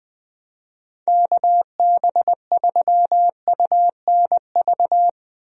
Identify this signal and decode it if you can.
Morse code, heard as a keyed tone, about 21 wpm, 700 Hz. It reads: KB3UNV
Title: USA Call Signs - 6 Characters - 20wpm